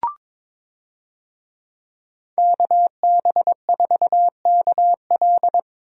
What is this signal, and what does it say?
Morse code, heard as a keyed tone, about 22 wpm, 700 Hz. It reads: KB4KL
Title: USA Call Signs - 5 Characters - Repeated 3x - 22wpm